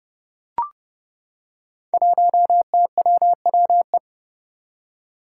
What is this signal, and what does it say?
Morse code, heard as a keyed tone, about 30 wpm, 700 Hz. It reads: 1TWWE
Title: Five Challenging Letter-Number Combinations 30wpm